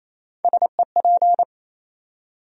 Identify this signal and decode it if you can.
Morse code, heard as a keyed tone, about 28 wpm, 700 Hz. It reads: SEP
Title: USA Call Sign Postfixes - 3 Letters 28wpm